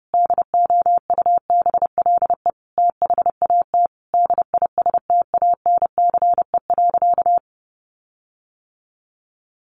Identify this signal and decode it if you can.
Morse code, heard as a keyed tone, about 30 wpm, 700 Hz. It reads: DOUBLE THAT DISTANCE.
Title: Sentences from Top 1400 Words 30wpm